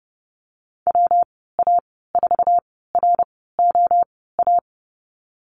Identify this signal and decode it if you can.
Morse code, heard as a keyed tone, about 30 wpm, spaced as for 17 wpm, 700 Hz. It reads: WA4ROA
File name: Lesson 19g - Letter w - Call Signs 30wpm